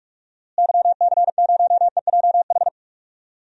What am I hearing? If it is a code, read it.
Morse code, heard as a keyed tone, about 42 wpm, 700 Hz. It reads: YC0EJH
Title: DXCC-100 Call Signs - 6 Characters - Repeated 3x - 45wpm